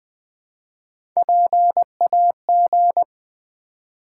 Morse code, heard as a keyed tone, about 20 wpm, 700 Hz. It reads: PAG
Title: Build A Word - 1 Letter At A Time - 4-Letter Words - 20wpm